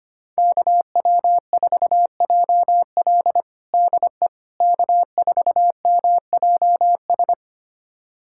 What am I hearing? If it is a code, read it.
Morse code, heard as a keyed tone, about 25 wpm, 700 Hz. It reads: KW4JL DE K4MJS
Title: USA Callsign DE Callsign - 5 Characters - 25wpm